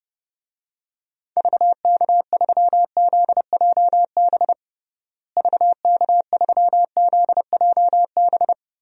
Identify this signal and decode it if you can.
Morse code, heard as a keyed tone, about 32 wpm, 700 Hz. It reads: VK3ZJB VK3ZJB
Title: DXCC-100 Call Signs - 6 Characters - Repeated 2x - 30wpm